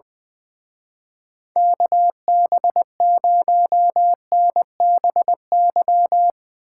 Morse code, heard as a keyed tone, about 22 wpm, 700 Hz. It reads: KB0NBY
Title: Speed Racing - USA 2x3 Call Signs 20wpm